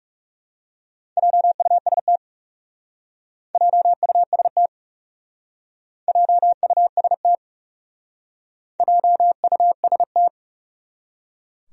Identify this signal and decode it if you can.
Morse code, heard as a keyed tone, about 38 wpm, 700 Hz. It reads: JUST JUST JUST JUST